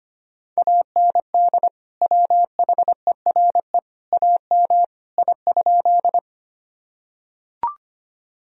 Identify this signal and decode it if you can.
Morse code, heard as a keyed tone, about 25 wpm, 700 Hz. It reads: AND WHERE AM I?
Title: Sentences from Top 1000 Words - Review Part 2 - 25wpm